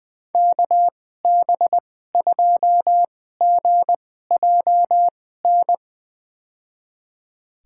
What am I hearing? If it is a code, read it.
Morse code, heard as a keyed tone, about 20 wpm, spaced as for 14 wpm, 700 Hz. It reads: KB2GJN
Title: USA Call Signs - 2x Character Spacing - 20wpm